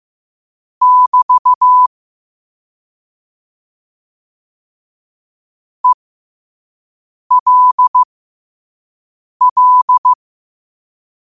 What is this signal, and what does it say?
Morse code, heard as a keyed tone, about 15 wpm, spaced as for 5 wpm, 1000 Hz. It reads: <BT> ELL
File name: qotdfiles / 5wpm